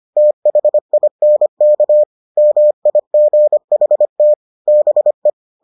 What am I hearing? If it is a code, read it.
Morse code, heard as a keyed tone, about 25 wpm, 600 Hz. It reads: THINK MIGHT BE